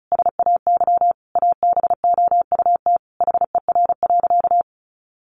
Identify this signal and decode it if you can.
Morse code, heard as a keyed tone, about 34 wpm, 700 Hz. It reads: SAY ABOUT HER.
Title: Sentences from Top 200 Words 35wpm